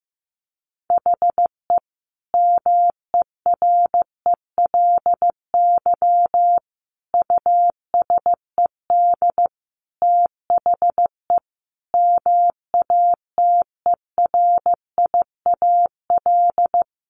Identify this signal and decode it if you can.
Morse code, heard as a keyed tone, about 14 wpm, 700 Hz. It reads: HE MERELY USED THE MATERIAL
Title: Sentences from Top 1400 Words 15wpm